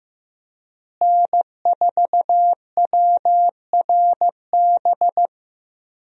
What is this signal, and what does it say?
Morse code, heard as a keyed tone, about 15 wpm, 700 Hz. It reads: N4WRB
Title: USA Call Signs - 5 Characters - Repeated 3x - 15wpm